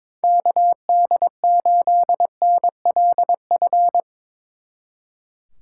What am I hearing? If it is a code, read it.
Morse code, heard as a keyed tone, about 24 wpm, 700 Hz. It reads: KD8NLF
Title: Speed Racing - USA 2x3 Call Signs 22wpm